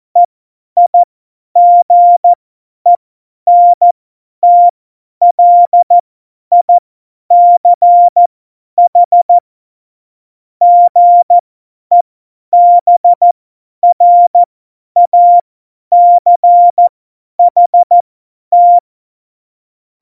Morse code, heard as a keyed tone, about 14 wpm, spaced as for 10 wpm, 700 Hz. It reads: EIGENTLICH GEBRACHT
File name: Morsekurs / Witze / Witz 01. Mai 2018, 50 F70